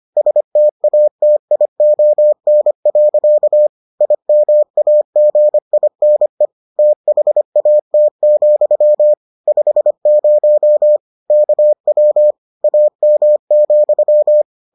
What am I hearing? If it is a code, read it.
Morse code, heard as a keyed tone, about 25 wpm, 600 Hz. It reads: STATION. IMAGINE THAT, 50 KW AM,